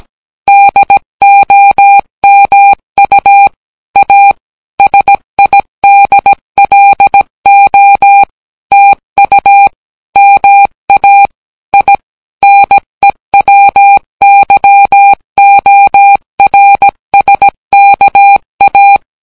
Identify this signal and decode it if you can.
Morse code, heard as a keyed tone, about 17 wpm, 800 Hz. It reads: DOMU A SIDLO TU MA I NEWYORSKA